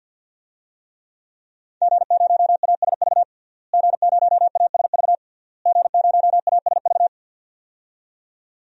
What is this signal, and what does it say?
Morse code, heard as a keyed tone, about 52 wpm, 700 Hz. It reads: G0ASV G0ASV G0ASV
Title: DXCC-100 Call Signs - 5 Characters - Repeated 3x - 50wpm